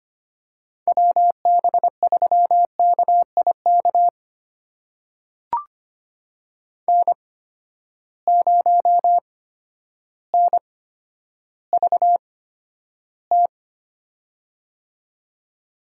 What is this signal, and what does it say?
Morse code, heard as a keyed tone, about 25 wpm, spaced as for 11 wpm, 700 Hz. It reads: WB3KIK N0NVT
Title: USA Call Signs - 8x Character Spacing - 25wpm